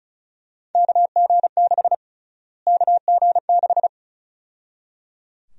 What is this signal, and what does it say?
Morse code, heard as a keyed tone, about 36 wpm, 700 Hz. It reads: KG6 KG6
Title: USA Call Sign Prefixes - Repeated 2x - 35wpm